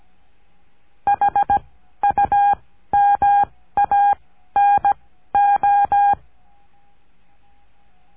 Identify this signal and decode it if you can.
Morse code, heard as a keyed tone, about 17 wpm, spaced as for 12 wpm, 810 Hz. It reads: HUMANO
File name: cursos / CURSO DE TELEGRAFIA / Telegrafia - Trilha 14